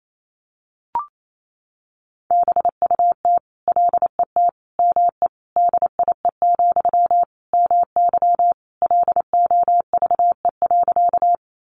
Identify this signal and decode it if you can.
Morse code, heard as a keyed tone, about 28 wpm, 700 Hz. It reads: BUT LET ME DIE, MY LOVE.
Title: Sentences from Top 1300 Words 28wpm